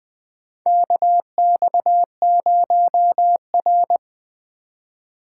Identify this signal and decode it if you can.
Morse code, heard as a keyed tone, about 20 wpm, 700 Hz. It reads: KX0R
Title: 2023 SOTA Chaser Call Signs - 20wpm